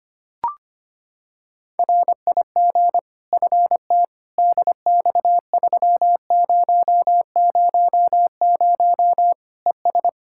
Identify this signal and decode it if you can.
Morse code, heard as a keyed tone, about 25 wpm, 700 Hz. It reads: RIG FT DX3000 ES
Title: K1USN SST QSO Practice 25wpm